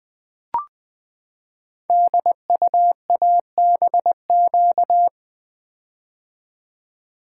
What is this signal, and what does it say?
Morse code, heard as a keyed tone, about 20 wpm, 700 Hz. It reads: DUABQ
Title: Code Groups - 5 Letters - 20wpm